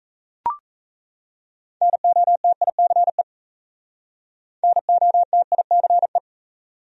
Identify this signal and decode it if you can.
Morse code, heard as a keyed tone, about 37 wpm, 700 Hz. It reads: NOTICE NOTICE